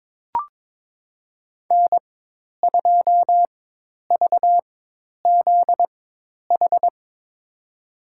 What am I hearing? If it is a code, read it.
Morse code, heard as a keyed tone, about 21 wpm, spaced as for 10 wpm, 700 Hz. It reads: N2VZH
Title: USA Call Signs - 5 Characters - 4x Character Spacing - 22wpm